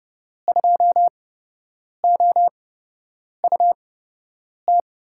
Morse code, heard as a keyed tone, about 30 wpm, spaced as for 8 wpm, 700 Hz. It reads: 2OUT
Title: USA Call Signs - 6 Characters - 8x Character Spacing - 30wpm